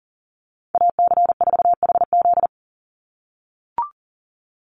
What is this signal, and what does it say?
Morse code, heard as a keyed tone, about 40 wpm, 690 Hz. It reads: AC4HZ
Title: USA Call Signs 40wpm